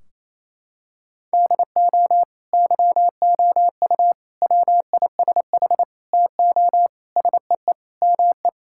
Morse code, heard as a keyed tone, about 28 wpm, 700 Hz. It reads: DO YOU WISH TO SEE ME
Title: Sentences from Top 1600 Words 28wpm